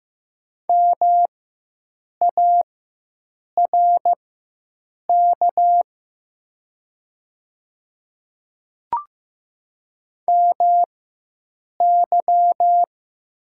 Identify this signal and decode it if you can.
Morse code, heard as a keyed tone, about 15 wpm, spaced as for 7 wpm, 690 Hz. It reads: MARK MY
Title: 4x Character Spacing - Top 500 Words - 15wpm